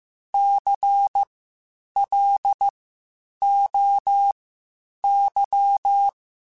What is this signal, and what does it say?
Morse code, heard as a keyed tone, about 15 wpm, spaced as for 8 wpm, 800 Hz. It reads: CLOY
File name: Spaced Alphas / Spaced Alphas mp3s / Cs sp @13